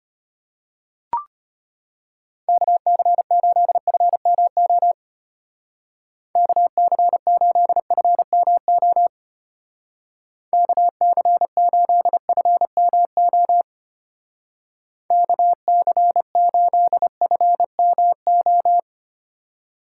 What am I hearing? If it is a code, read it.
Morse code, heard as a keyed tone, about 32 wpm, 700 Hz. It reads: KC8FMO KC8FMO KC8FMO KC8FMO